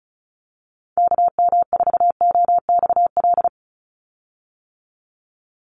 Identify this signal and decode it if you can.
Morse code, heard as a keyed tone, about 34 wpm, 700 Hz. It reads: KM4OXL
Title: USA Call Signs 35wpm